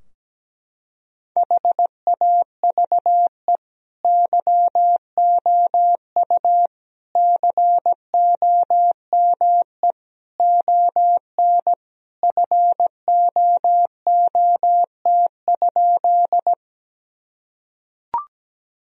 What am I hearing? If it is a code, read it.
Morse code, heard as a keyed tone, about 17 wpm, 700 Hz. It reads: HAVE YOU COME ON FOOT?
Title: Sentences from Top 1000 Words - Review Part 3 - 17wpm